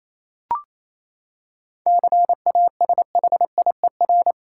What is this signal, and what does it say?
Morse code, heard as a keyed tone, about 28 wpm, 700 Hz. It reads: CASHIER
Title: Practice QSO - Exchange 3 - More Info - 28wpm